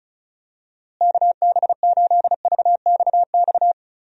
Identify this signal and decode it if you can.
Morse code, heard as a keyed tone, about 36 wpm, 700 Hz. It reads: KB8VXX
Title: USA Call Signs - 6 Characters - Repeated 2x - 35wpm